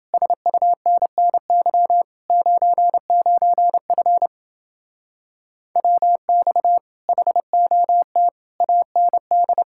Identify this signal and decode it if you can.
Morse code, heard as a keyed tone, about 29 wpm, 700 Hz. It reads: SUNNY 99F WX HOT AND